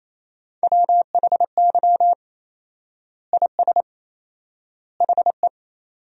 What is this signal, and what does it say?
Morse code, heard as a keyed tone, about 28 wpm, 700 Hz. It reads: WHY IS HE
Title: 4x Word-Spacing - Sentences from Top 200 Words - 28wpm